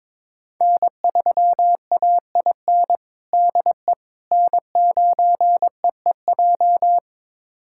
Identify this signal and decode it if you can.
Morse code, heard as a keyed tone, about 22 wpm, 700 Hz. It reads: N3AIN DE N9EEJ
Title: USA Callsign DE Callsign 22wpm